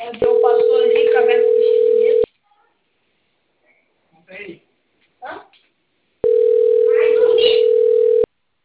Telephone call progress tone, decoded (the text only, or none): ringback tone